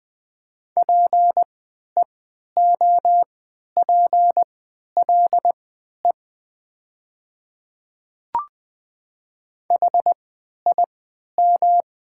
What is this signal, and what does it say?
Morse code, heard as a keyed tone, about 20 wpm, spaced as for 11 wpm, 700 Hz. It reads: PEOPLE HIM
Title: 3x Character Spacing - Top 200 Words - 20wpm